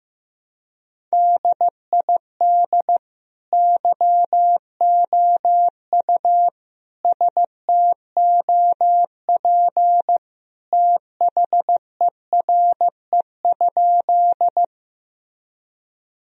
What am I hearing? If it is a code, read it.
Morse code, heard as a keyed tone, about 15 wpm, 700 Hz. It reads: DID YOU STOP THERE?
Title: Sentences from Top 1000 Words - Review Part 1 - 15wpm